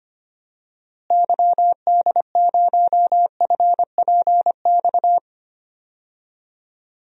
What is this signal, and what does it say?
Morse code, heard as a keyed tone, about 25 wpm, 700 Hz. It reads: YD0FPX
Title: DXCC-100 Call Signs - 6 Characters - Repeated 3x - 25wpm